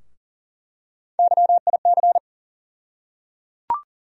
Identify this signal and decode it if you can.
Morse code, heard as a keyed tone, about 38 wpm, 700 Hz. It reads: YIC
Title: USA Call Sign Postfixes - 3 Letters 40wpm